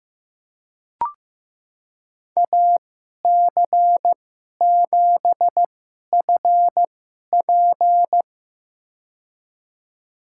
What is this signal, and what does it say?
Morse code, heard as a keyed tone, about 15 wpm, spaced as for 11 wpm, 700 Hz. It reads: AC7FP
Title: USA Call Signs - 2x Character Spacing - 15wpm